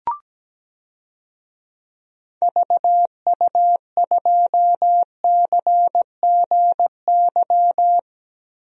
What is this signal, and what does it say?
Morse code, heard as a keyed tone, about 17 wpm, 700 Hz. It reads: VU2CGY
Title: DXCC-100 Call Signs - 6 Characters - 17wpm